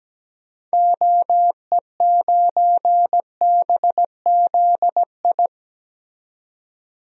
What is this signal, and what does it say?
Morse code, heard as a keyed tone, about 17 wpm, 700 Hz. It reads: OE9BZI
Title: DXCC-100 Call Signs - 6 Characters - 17wpm